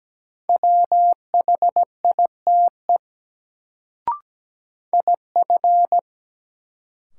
Morse code, heard as a keyed tone, about 17 wpm, 700 Hz. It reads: WHITE IF
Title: Sets of 2 Words - New Word Drill - 17wpm